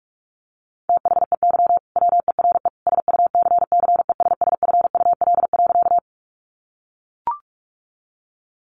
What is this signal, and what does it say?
Morse code, heard as a keyed tone, about 44 wpm, 700 Hz. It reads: THEY WERE SUCCESSFUL.
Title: Sentences from Top 900 Words 45wpm